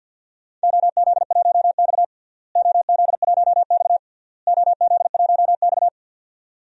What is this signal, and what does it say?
Morse code, heard as a keyed tone, about 51 wpm, 700 Hz. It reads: OZ1X OZ1X OZ1X
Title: DXCC-100 Call Signs - 4 Characters - Repeated 3x - 50wpm